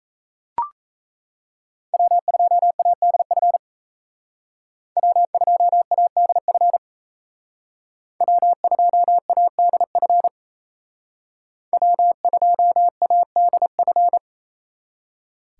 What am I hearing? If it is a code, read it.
Morse code, heard as a keyed tone, about 36 wpm, 700 Hz. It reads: W2ADF W2ADF W2ADF W2ADF